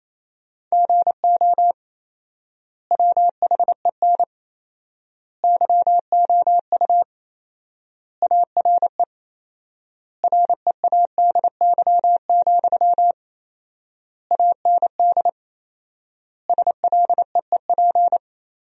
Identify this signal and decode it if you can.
Morse code, heard as a keyed tone, about 28 wpm, 700 Hz. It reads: GO WHEN YOU ARE READY, AND SLEEP
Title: 4x Word-Spacing - Sentences from Top 500 Words - 28wpm